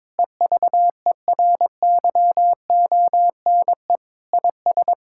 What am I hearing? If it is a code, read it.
Morse code, heard as a keyed tone, about 22 wpm, 700 Hz. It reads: EVERYONE IS
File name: practice / 22wpm / Sentences with 8-Word Phrases - 22wpm